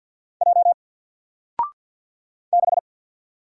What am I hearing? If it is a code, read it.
Morse code, heard as a keyed tone, about 48 wpm, 700 Hz. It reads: J 6
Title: Single Letter-Number Alt 50wpm